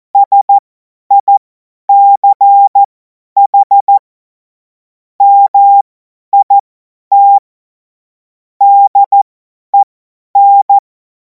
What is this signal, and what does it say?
Morse code, heard as a keyed tone, about 14 wpm, spaced as for 10 wpm, 800 Hz. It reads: SICH MIT DEN
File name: Morsekurs / Witze / Witz 02. Juni 2026 50 F70